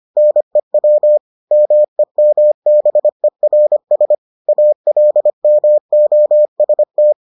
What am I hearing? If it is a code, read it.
Morse code, heard as a keyed tone, about 25 wpm, 600 Hz. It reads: NEW MEMBERS ALMOST